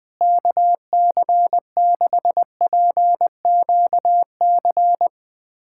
Morse code, heard as a keyed tone, about 20 wpm, 700 Hz. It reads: KC6PQC
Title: USA Callsign DE Callsign 20wpm